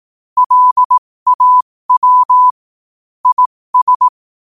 Morse code, heard as a keyed tone, about 18 wpm, spaced as for 15 wpm, 1000 Hz. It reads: LAW IS
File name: qotdfiles / 15wpm